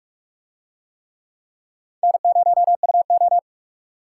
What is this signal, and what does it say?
Morse code, heard as a keyed tone, about 44 wpm, 700 Hz. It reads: N0UO